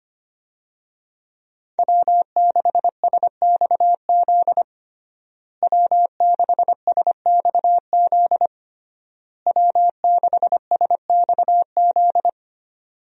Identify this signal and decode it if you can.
Morse code, heard as a keyed tone, about 25 wpm, 700 Hz. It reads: W6SXZ W6SXZ W6SXZ
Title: USA Call Signs - 5 Characters - Repeated 3x - 25wpm